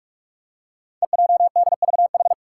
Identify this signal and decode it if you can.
Morse code, heard as a keyed tone, about 44 wpm, 700 Hz. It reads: EJDVH
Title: Five Challenging Letter-Number Combinations 45wpm